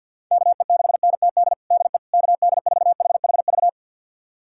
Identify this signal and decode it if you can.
Morse code, heard as a keyed tone, about 52 wpm, 700 Hz. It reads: KE6NTD DE KD4HHV
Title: USA Callsign DE Callsign 50wpm